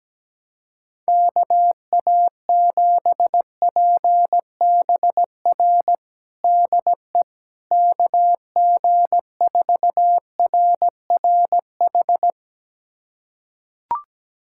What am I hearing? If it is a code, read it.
Morse code, heard as a keyed tone, about 17 wpm, 700 Hz. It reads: KA7PBR DE KG4RRH
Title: USA Callsign DE Callsign - 6 Characters - 17wpm